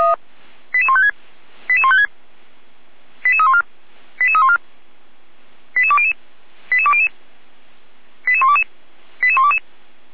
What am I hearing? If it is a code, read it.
Five-tone selective call (ZVEI), heard as quick runs of five tones, tones of about 70 ms. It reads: 88156 88156 88314 88314 88200 88200 88122 88122